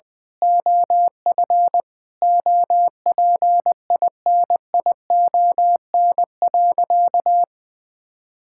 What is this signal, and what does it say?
Morse code, heard as a keyed tone, about 20 wpm, 700 Hz. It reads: OF OPINION.
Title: Sentences from Top 1600 Words 20wpm